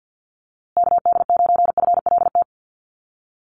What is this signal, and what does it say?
Morse code, heard as a keyed tone, about 50 wpm, 700 Hz. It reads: KD9FLT
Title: USA Call Signs Repeated 3x 50wpm